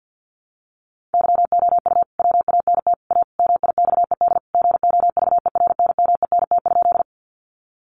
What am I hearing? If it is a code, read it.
Morse code, heard as a keyed tone, about 50 wpm, 700 Hz. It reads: YOU WANT A MIXED GOVERNMENT?
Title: Sentences from Top 1500 Words 50wpm